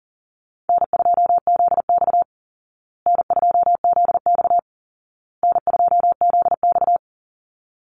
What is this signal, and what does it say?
Morse code, heard as a keyed tone, about 40 wpm, 700 Hz. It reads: N2ZX N2ZX N2ZX
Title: Easy Call Signs Repeated 3x - 40wpm